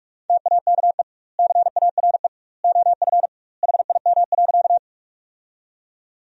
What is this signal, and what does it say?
Morse code, heard as a keyed tone, about 46 wpm, 700 Hz. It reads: TAKE CARE OF HIM.
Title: 2x Word-Spacing - Sentences from Top 300 Words - 45wpm